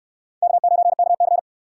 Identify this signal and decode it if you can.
Morse code, heard as a keyed tone, about 69 wpm, 700 Hz. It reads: K9KC